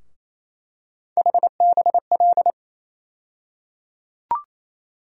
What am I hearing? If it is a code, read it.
Morse code, heard as a keyed tone, about 28 wpm, 700 Hz. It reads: HBL
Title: USA Call Sign Postfixes - 3 Letters 28wpm